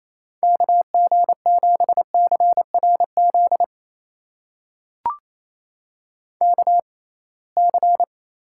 Morse code, heard as a keyed tone, about 28 wpm, spaced as for 21 wpm, 700 Hz. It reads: KG7CRZ KC
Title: USA Call Signs - 6 Characters - 6x Character Spacing - 28wpm